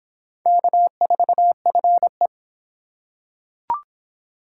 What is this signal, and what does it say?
Morse code, heard as a keyed tone, about 24 wpm, 700 Hz. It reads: K4FE